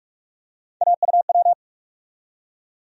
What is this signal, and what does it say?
Morse code, heard as a keyed tone, about 46 wpm, 700 Hz. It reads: AUW
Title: USA Call Sign Postfixes - 3 Letters 45wpm